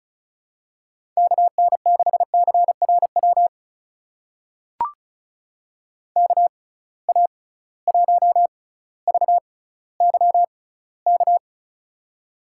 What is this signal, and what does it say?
Morse code, heard as a keyed tone, about 36 wpm, spaced as for 18 wpm, 700 Hz. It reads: KN6CRW KA1VYK
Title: USA Call Signs - 6 Characters - 6x Character Spacing - 35wpm